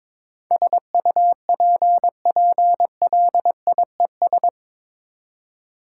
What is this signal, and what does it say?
Morse code, heard as a keyed tone, about 21 wpm, 700 Hz. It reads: SUPPLIES
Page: morse-code-ninja-practice-22wpm